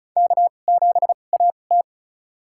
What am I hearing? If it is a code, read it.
Morse code, heard as a keyed tone, about 36 wpm, spaced as for 25 wpm, 700 Hz. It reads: K7AT
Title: USA Call Signs - 4 Characters - 2x Character Spacing - 35wpm